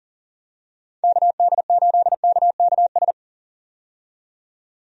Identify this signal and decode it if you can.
Morse code, heard as a keyed tone, about 40 wpm, 700 Hz. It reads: KD8KKS
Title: USA Call Signs - 6 Characters - 40wpm